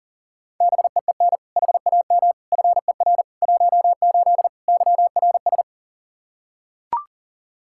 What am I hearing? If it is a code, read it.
Morse code, heard as a keyed tone, about 40 wpm, 700 Hz. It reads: BEEN HAM FER 18 YRS
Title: Practice QSO - Exchange 3 - More Info - 40wpm